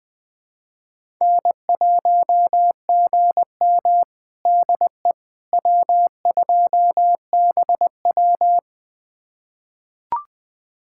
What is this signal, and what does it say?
Morse code, heard as a keyed tone, about 20 wpm, 700 Hz. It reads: N1GM DE W2BW
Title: USA Callsign DE Callsign - 4 Characters - 20wpm